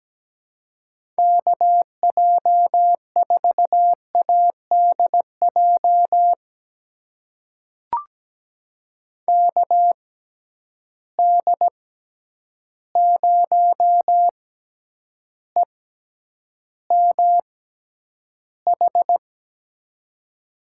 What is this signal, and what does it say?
Morse code, heard as a keyed tone, about 17 wpm, spaced as for 9 wpm, 700 Hz. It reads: KJ4ADJ KD0EMH
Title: USA Call Signs - 6 Characters - 6x Character Spacing - 17wpm